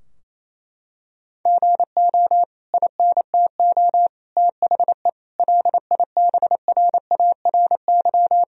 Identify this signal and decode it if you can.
Morse code, heard as a keyed tone, about 28 wpm, 700 Hz. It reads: GO INTO THE LIBRARY
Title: Sentences from Top 1000 Words - Review Part 4 - 28wpm